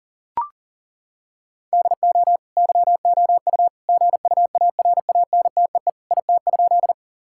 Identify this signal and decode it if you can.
Morse code, heard as a keyed tone, about 40 wpm, 700 Hz. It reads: DO YOU GUARANTEE IT?
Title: Sentences from Top 1600 Words 40wpm